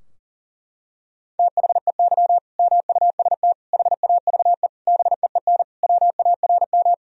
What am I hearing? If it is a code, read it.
Morse code, heard as a keyed tone, about 40 wpm, 700 Hz. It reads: THEY MUST HAVE BEEN WARM
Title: Sentences from Top 1000 Words - Review Part 2 - 40wpm